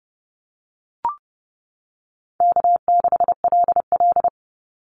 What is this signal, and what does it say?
Morse code, heard as a keyed tone, about 30 wpm, 700 Hz. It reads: K6LL
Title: Call Signs - 30wpm